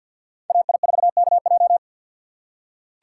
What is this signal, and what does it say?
Morse code, heard as a keyed tone, about 46 wpm, 700 Hz. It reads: AI4KJ
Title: USA Call Signs - 5 Characters - 6x Character Spacing - 50wpm